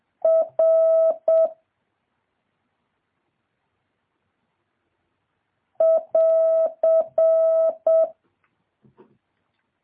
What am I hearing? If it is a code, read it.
Morse code, heard as a keyed tone, about 7 wpm, 650 Hz. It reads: R <AR>
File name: jh8cbh / 2023cw